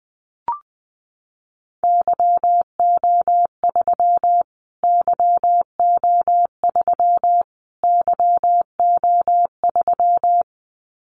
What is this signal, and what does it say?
Morse code, heard as a keyed tone, about 20 wpm, 700 Hz. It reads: YO3 YO3 YO3
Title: Callsign Prefixes for Top 25 DXCC - Repeated 3x - 20wpm